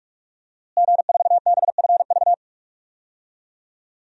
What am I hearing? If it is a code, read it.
Morse code, heard as a keyed tone, about 46 wpm, 700 Hz. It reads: G4BFV
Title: DXCC-100 Call Signs - 5 Characters - Repeated 2x - 45wpm